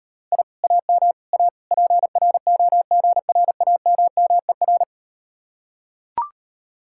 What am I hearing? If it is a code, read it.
Morse code, heard as a keyed tone, about 39 wpm, 700 Hz. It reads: I AM A PROGRAMMER